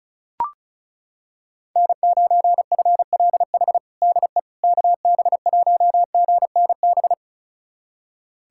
Morse code, heard as a keyed tone, about 34 wpm, 700 Hz. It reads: N9FLH DE KB1GNB
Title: USA Callsign DE Callsign 35wpm